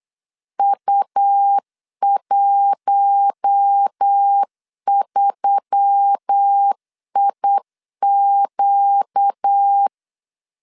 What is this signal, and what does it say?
Morse code, heard as a keyed tone, about 8 wpm, 800 Hz. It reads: U13IQ
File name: CW / DET / 1 6 1500C S B 8wpm 5142